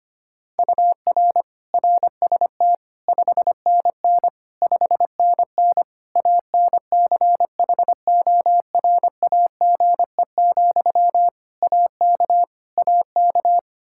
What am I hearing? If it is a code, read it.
Morse code, heard as a keyed tone, about 25 wpm, 700 Hz. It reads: UR RST 5NN 5NN ANCHORAGE, AK AK
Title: Practice QSO - Exchange 1 - Beginning - 25wpm